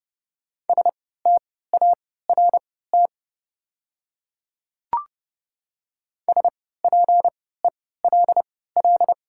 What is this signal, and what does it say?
Morse code, heard as a keyed tone, about 30 wpm, spaced as for 17 wpm, 700 Hz. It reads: START SPELL
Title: 3x Character Spacing - Top 300 Words - 30wpm